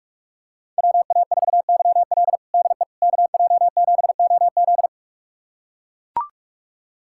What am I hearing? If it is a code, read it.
Morse code, heard as a keyed tone, about 42 wpm, 700 Hz. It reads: WA4YL DE KJ7OZ
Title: USA Callsign DE Callsign 45wpm